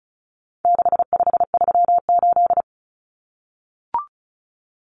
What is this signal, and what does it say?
Morse code, heard as a keyed tone, about 34 wpm, 700 Hz. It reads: 6538
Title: Common Cut Numbers - 4-Digit 35wpm